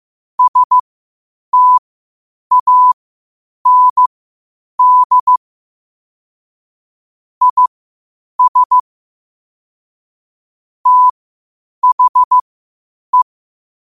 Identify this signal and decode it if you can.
Morse code, heard as a keyed tone, about 15 wpm, spaced as for 8 wpm, 1000 Hz. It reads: STAND IS THE
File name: qotdfiles / 8wpm